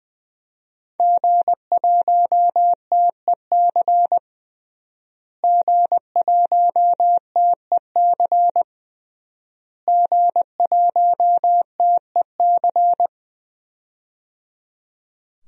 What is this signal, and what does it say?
Morse code, heard as a keyed tone, about 20 wpm, 700 Hz. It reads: G1TEC G1TEC G1TEC
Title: DXCC-100 Call Signs - 5 Characters - Repeated 3x - 20wpm